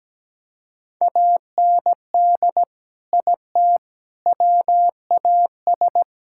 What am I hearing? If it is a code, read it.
Morse code, heard as a keyed tone, about 17 wpm, 700 Hz. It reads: AND IT WAS
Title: Sentences from Top 1600 Words 17wpm